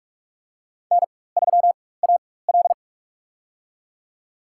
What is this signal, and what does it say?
Morse code, heard as a keyed tone, about 46 wpm, spaced as for 21 wpm, 700 Hz. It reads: N3AL
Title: USA Call Signs - 4x Character Spacing - 45wpm